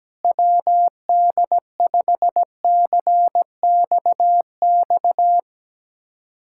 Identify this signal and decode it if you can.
Morse code, heard as a keyed tone, about 17 wpm, 700 Hz. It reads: WD5CXX
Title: USA Callsign DE Callsign 17wpm